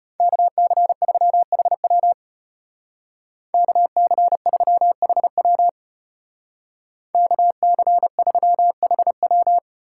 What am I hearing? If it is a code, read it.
Morse code, heard as a keyed tone, about 31 wpm, 700 Hz. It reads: KC3HW KC3HW KC3HW
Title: Speed Racing - USA 2x2 Call Signs 25wpm